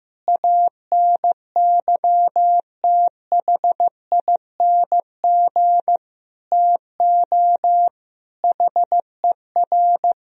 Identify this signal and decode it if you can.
Morse code, heard as a keyed tone, about 16 wpm, 700 Hz. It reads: ANYTHING TO HER
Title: Sentences from Top 1000 Words - Review Part 4 - 15wpm